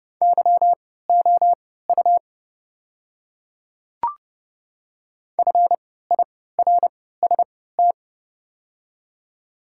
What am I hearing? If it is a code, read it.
Morse code, heard as a keyed tone, about 30 wpm, spaced as for 17 wpm, 700 Hz. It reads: YOU FIRST
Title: 3x Character Spacing - Top 100 Words - 30wpm